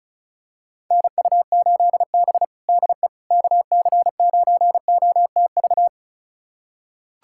Morse code, heard as a keyed tone, about 36 wpm, 700 Hz. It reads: NU8B DE KC9OTV
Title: USA Callsign DE Callsign 35wpm